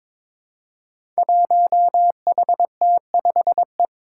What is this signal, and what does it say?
Morse code, heard as a keyed tone, about 22 wpm, 700 Hz. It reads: 1HT5E
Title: Five Challenging Letter-Number Combinations 22wpm